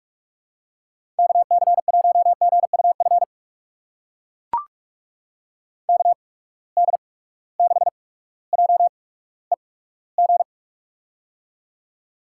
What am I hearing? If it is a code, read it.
Morse code, heard as a keyed tone, about 44 wpm, spaced as for 19 wpm, 700 Hz. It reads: KC1GUF KD6JEG
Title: USA Call Signs - 8x Character Spacing - 45wpm